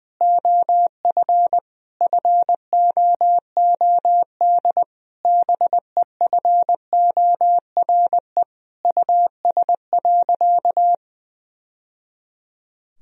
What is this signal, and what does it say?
Morse code, heard as a keyed tone, about 20 wpm, 700 Hz. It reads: OF FOOD BEFORE US.
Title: Sentences from Top 1000 Words - Review Part 4 - 20wpm